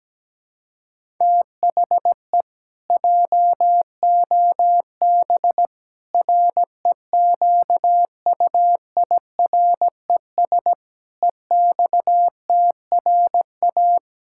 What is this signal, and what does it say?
Morse code, heard as a keyed tone, about 17 wpm, 700 Hz. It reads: THE JOB REQUIRES EXTRA
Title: Pangram Challenge v2 - 17wpm